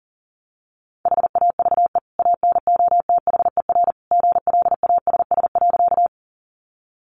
Morse code, heard as a keyed tone, about 40 wpm, 700 Hz. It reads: HAVE ANOTHER GLASS.
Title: Sentences from Top 1300 Words 40wpm